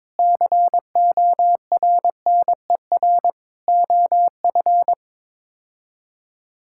Morse code, heard as a keyed tone, about 22 wpm, 700 Hz. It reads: CORNER OF
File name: practice / 22wpm / Build A Phrase - 5-Word Phrases - 22wpm